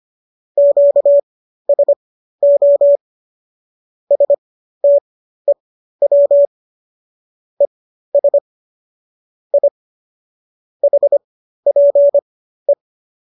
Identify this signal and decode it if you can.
Morse code, heard as a keyed tone, about 25 wpm, spaced as for 13 wpm, 570 Hz. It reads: QSO STEW ES I HPE